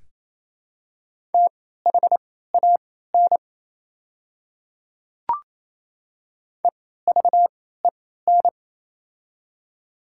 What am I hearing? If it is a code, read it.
Morse code, heard as a keyed tone, about 28 wpm, spaced as for 16 wpm, 700 Hz. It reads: THAN EVEN
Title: 3x Character Spacing - Top 200 Words - 28wpm